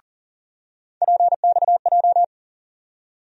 Morse code, heard as a keyed tone, about 40 wpm, 700 Hz. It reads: PXJ
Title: USA Call Sign Postfixes - 3 Letters 40wpm